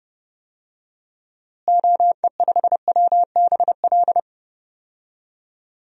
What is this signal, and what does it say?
Morse code, heard as a keyed tone, about 30 wpm, 700 Hz. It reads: OE5WBL
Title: DXCC-100 Call Signs - 6 Characters - Repeated 2x - 30wpm